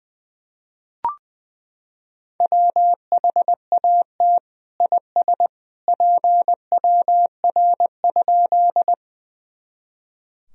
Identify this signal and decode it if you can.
Morse code, heard as a keyed tone, about 20 wpm, 700 Hz. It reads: WHAT IS PWR?
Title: Sentences from Top 500 Words and Abbreviations 20wpm